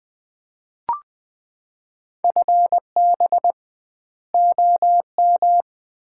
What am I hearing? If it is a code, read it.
Morse code, heard as a keyed tone, about 21 wpm, 700 Hz. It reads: FB OM
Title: Common QSO Phrases - 2x Word-Spacing - 20wpm